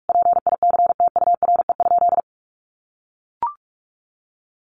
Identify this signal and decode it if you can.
Morse code, heard as a keyed tone, about 44 wpm, 700 Hz. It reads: PICTURE?
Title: A Sound Of Thunder 45wpm